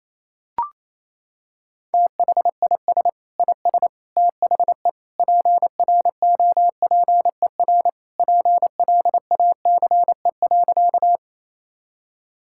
Morse code, heard as a keyed tone, about 27 wpm, 700 Hz. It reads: THIS IS THE PROPER PLACE.
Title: Sentences from Top 1000 Words - Review Part 5 - 28wpm